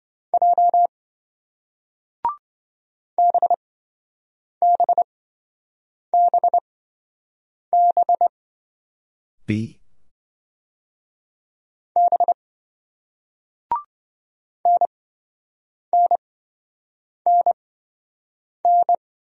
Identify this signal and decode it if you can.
Morse code, heard as a keyed tone, about 27 wpm, spaced as for 7 wpm, 700 Hz. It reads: J BBBB B NNNN